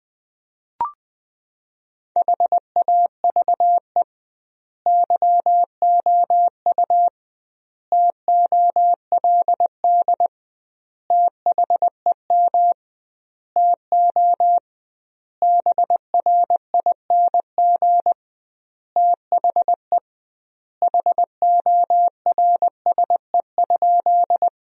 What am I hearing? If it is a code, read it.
Morse code, heard as a keyed tone, about 20 wpm, 700 Hz. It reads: HAVE YOU TOLD THEM TO BRING THE HORSE?
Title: 2x Word-Spacing - Sentences from Top 500 Words - 20wpm